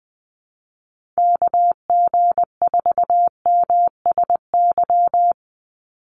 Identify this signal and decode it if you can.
Morse code, heard as a keyed tone, about 20 wpm, 700 Hz. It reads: KG4MSY
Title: USA Call Signs 20wpm